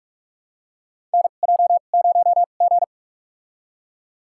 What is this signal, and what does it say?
Morse code, heard as a keyed tone, about 46 wpm, spaced as for 32 wpm, 700 Hz. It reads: NJ0G
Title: USA Call Signs - 4 Characters - 2x Character Spacing - 45wpm